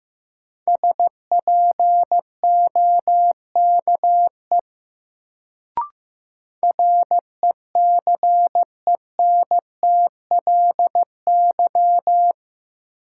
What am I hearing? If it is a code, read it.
Morse code, heard as a keyed tone, about 15 wpm, 700 Hz. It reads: SPOKE RECENTLY
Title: Sentences from Top 1200 Words - New Word Drill Alt - 15wpm